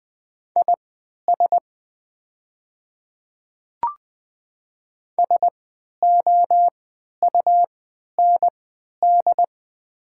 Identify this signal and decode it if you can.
Morse code, heard as a keyed tone, about 20 wpm, spaced as for 11 wpm, 700 Hz. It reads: IS SOUND
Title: 3x Character Spacing - Top 100 Words - 20wpm